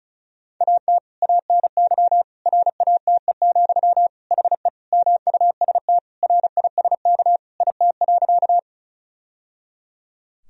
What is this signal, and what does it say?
Morse code, heard as a keyed tone, about 34 wpm, 700 Hz. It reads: AT ANY RATE, HE MUST RISK IT.
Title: Sentences from Top 1000 Words - Review Part 4 - 35wpm